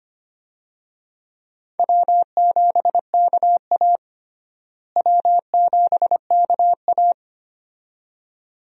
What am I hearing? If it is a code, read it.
Morse code, heard as a keyed tone, about 25 wpm, 700 Hz. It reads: W7KA W7KA
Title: USA Call Signs - 4 Characters - Repeated 2x - 25wpm